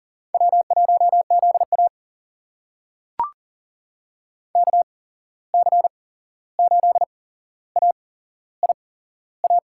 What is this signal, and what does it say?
Morse code, heard as a keyed tone, about 40 wpm, spaced as for 15 wpm, 700 Hz. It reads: W1ZA KC8AIA
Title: USA Call Signs - 8x Character Spacing - 40wpm